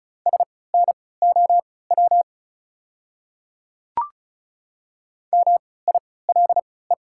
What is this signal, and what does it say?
Morse code, heard as a keyed tone, about 34 wpm, spaced as for 20 wpm, 690 Hz. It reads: SNOW MILE
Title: 3x Character Spacing - Top 500 Words - 35wpm